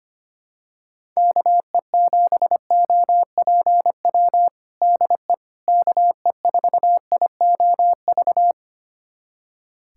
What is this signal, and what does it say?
Morse code, heard as a keyed tone, about 25 wpm, 700 Hz. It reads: KE7OPW DE KE4IOV
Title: USA Callsign DE Callsign - 6 Characters - 25wpm